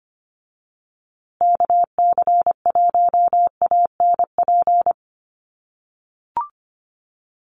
Morse code, heard as a keyed tone, about 25 wpm, 700 Hz. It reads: KC1ANP
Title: USA Call Signs Repeated 2x 25wpm